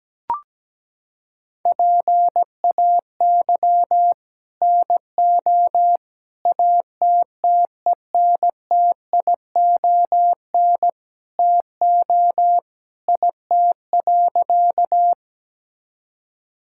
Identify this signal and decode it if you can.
Morse code, heard as a keyed tone, about 17 wpm, 700 Hz. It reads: PAY NO ATTENTION TO IT.
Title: Sentences from Top 1000 Words - Review Part 6 - 17wpm